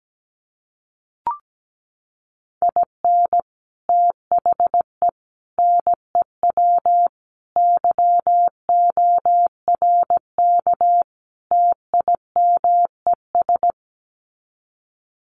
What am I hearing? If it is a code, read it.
Morse code, heard as a keyed tone, about 17 wpm, 700 Hz. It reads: IN THE NEW YORK TIMES
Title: Sets of 5 Words 17wpm